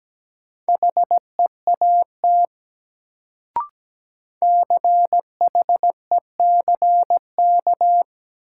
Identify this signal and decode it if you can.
Morse code, heard as a keyed tone, about 17 wpm, 700 Hz. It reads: HEAT CHECK
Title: Sentences from Top 1000 Words - Review Part 3 - New Word Drill - 17wpm